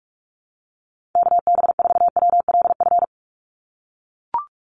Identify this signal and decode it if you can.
Morse code, heard as a keyed tone, about 44 wpm, 700 Hz. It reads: KB4WLF
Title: Call Signs - 45wpm